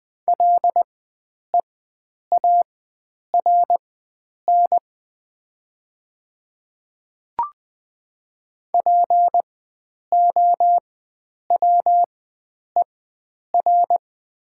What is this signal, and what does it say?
Morse code, heard as a keyed tone, about 20 wpm, spaced as for 9 wpm, 700 Hz. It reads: LEARN POWER
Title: 4x Character Spacing - Top 500 Words - 20wpm